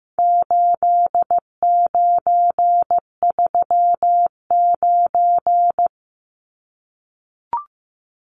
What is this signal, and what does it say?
Morse code, heard as a keyed tone, about 15 wpm, 700 Hz. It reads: 8939
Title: Numbers - 4-Digit 15wpm